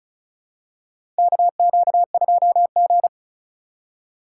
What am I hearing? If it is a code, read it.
Morse code, heard as a keyed tone, about 36 wpm, 700 Hz. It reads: KQ2G
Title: USA Call Signs - 2x Character Spacing - 35wpm